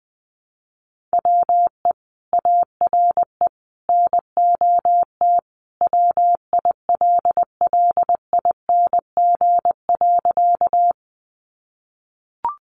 Morse code, heard as a keyed tone, about 20 wpm, 700 Hz. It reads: WE ARE NOT WILLING.
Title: Sentences from Top 1000 Words 20wpm